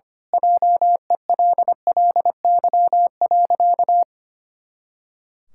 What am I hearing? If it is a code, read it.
Morse code, heard as a keyed tone, about 25 wpm, 700 Hz. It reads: JELLY.
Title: Pangram Challenge v2 - 25wpm